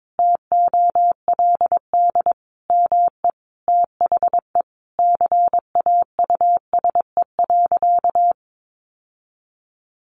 Morse code, heard as a keyed tone, about 22 wpm, 700 Hz. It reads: TOLD ME THE CAUSE.
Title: Sentences from Top 700 Words 22wpm